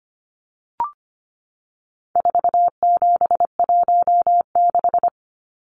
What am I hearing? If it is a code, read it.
Morse code, heard as a keyed tone, about 25 wpm, 700 Hz. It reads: 4716
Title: Numbers - 4-Digit 25wpm